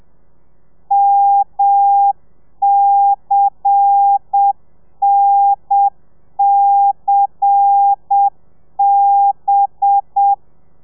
Morse code, heard as a keyed tone, about 7 wpm, 800 Hz. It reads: MCNCB
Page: Lezione 1: audio a 30 CPM